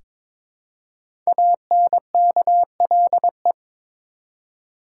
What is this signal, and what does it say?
Morse code, heard as a keyed tone, about 22 wpm, 700 Hz. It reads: ANKLE
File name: 5 Letter Words - First 3 Letters Given - 22wpm